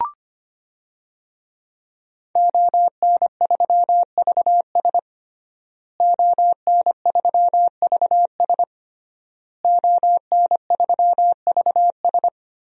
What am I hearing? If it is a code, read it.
Morse code, heard as a keyed tone, about 25 wpm, 700 Hz. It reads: ON3VS ON3VS ON3VS
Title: DXCC-100 Call Signs - 5 Characters - Repeated 3x - 25wpm